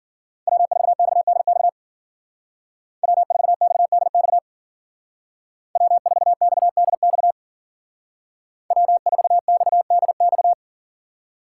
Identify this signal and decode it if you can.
Morse code, heard as a keyed tone, about 49 wpm, 700 Hz. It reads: W4XDX W4XDX W4XDX W4XDX
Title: Speed Racing - USA 1x3 Call Signs 40wpm